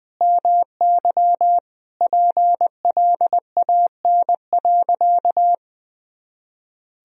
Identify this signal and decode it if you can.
Morse code, heard as a keyed tone, about 20 wpm, 700 Hz. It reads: MY PLAN.
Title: Sentences from Top 500 Words and Abbreviations 20wpm